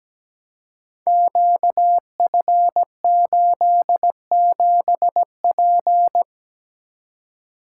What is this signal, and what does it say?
Morse code, heard as a keyed tone, about 17 wpm, 700 Hz. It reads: QF87P
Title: Code Groups - 5 Letter:Numbers - 17wpm